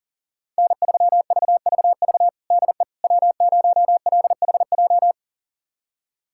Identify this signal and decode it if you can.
Morse code, heard as a keyed tone, about 40 wpm, 700 Hz. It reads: N3VVV DE W0LHJ
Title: USA Callsign DE Callsign - 5 Characters - 40wpm